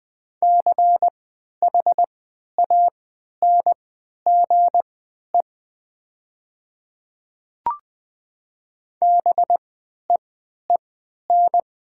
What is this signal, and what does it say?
Morse code, heard as a keyed tone, about 20 wpm, spaced as for 11 wpm, 700 Hz. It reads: CHANGE BEEN
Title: 3x Character Spacing - Top 300 Words - 20wpm